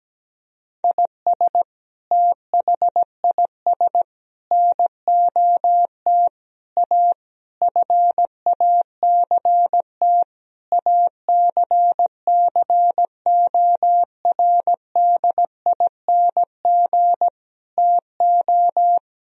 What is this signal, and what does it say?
Morse code, heard as a keyed tone, about 17 wpm, 700 Hz. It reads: IS THIS NOT A FACT ACCORDING TO
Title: Sentences from Top 1000 Words - Review Part 4 - 17wpm